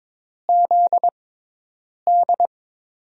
Morse code, heard as a keyed tone, about 22 wpm, spaced as for 8 wpm, 700 Hz. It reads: ZD
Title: USA Call Signs - 5 Characters - 6x Character Spacing - 22wpm